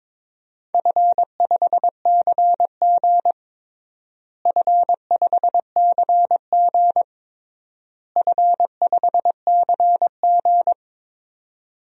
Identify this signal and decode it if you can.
Morse code, heard as a keyed tone, about 22 wpm, 700 Hz. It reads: F5CG F5CG F5CG
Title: DXCC-100 Call Signs - 4 Characters - Repeated 3x - 22wpm